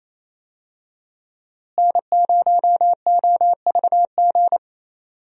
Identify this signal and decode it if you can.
Morse code, heard as a keyed tone, about 27 wpm, 700 Hz. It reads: N0OVG
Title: USA Call Signs - 5 Characters - Repeated 3x - 28wpm